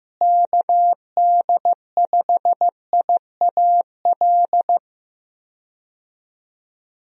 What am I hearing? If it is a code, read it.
Morse code, heard as a keyed tone, about 15 wpm, 700 Hz. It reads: KD5IAL
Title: USA Call Signs - 6 Characters - Repeated 2x - 15wpm